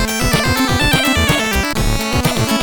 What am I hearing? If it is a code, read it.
There is no code.